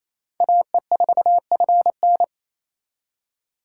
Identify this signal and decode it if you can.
Morse code, heard as a keyed tone, about 35 wpm, 700 Hz. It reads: AE4FN
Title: Speed Racing - USA 2x2 Call Signs 28wpm